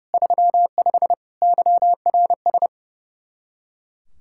Morse code, heard as a keyed tone, about 37 wpm, 700 Hz. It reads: 35 YRS
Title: Speed-Racing Common QSO Phrases Encore - 30wpm